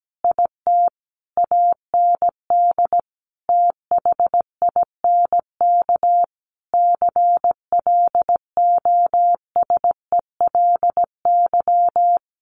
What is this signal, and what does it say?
Morse code, heard as a keyed tone, about 17 wpm, 700 Hz. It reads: IT AND THINK CLOSELY
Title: Sentences from Top 1400 Words 17wpm